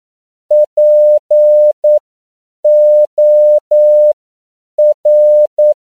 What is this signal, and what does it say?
Morse code, heard as a keyed tone, about 9 wpm, spaced as for 7 wpm, 600 Hz. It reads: POR
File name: CW / 10PPM